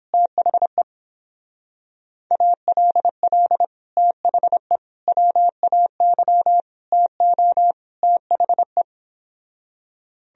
Morse code, heard as a keyed tone, about 29 wpm, 700 Hz. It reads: THE ALL THE WAY TO THE